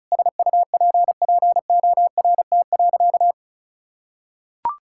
Morse code, heard as a keyed tone, about 36 wpm, 700 Hz. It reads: SUPPORT.
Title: Sentences from Top 1000 Words - Review Part 5 - 35wpm